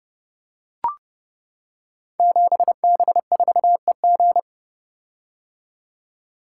Morse code, heard as a keyed tone, about 28 wpm, 700 Hz. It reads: 7B4EG
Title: Code Groups - 5 Letter:Numbers - 30wpm